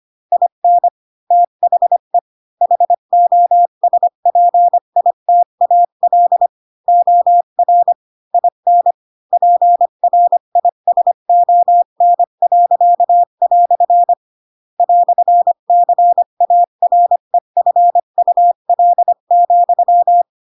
Morse code, heard as a keyed tone, about 25 wpm, 700 Hz. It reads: IN THE HOSPITAL OR IN PRISON." "CAREFUL,
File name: Pinocchio / Pinocchio ch4 20 25 ik1hge